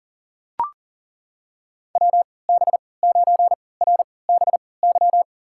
Morse code, heard as a keyed tone, about 40 wpm, spaced as for 23 wpm, 700 Hz. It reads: WB9RBY
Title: USA Call Signs - 3x Character Spacing - 40wpm